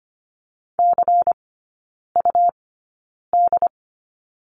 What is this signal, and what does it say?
Morse code, heard as a keyed tone, about 25 wpm, spaced as for 9 wpm, 700 Hz. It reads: CUD
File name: Lesson 17e - Letter u - Three Characters 25wpm